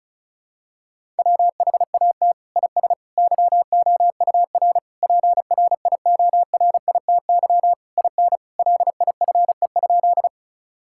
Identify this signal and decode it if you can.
Morse code, heard as a keyed tone, about 36 wpm, 700 Hz. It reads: WHAT IS YOUR PRIORITY IN LIFE?
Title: Sentences from Top 1600 Words 35wpm